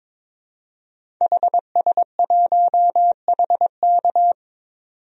Morse code, heard as a keyed tone, about 22 wpm, 700 Hz. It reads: HS1HK
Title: DXCC-100 Call Signs - 5 Characters - 22wpm